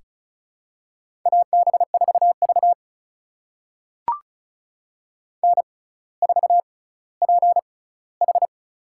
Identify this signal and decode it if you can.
Morse code, heard as a keyed tone, about 34 wpm, spaced as for 18 wpm, 700 Hz. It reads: AB4V N4PH
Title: USA Call Signs - 4 Characters - 6x Character Spacing - 35wpm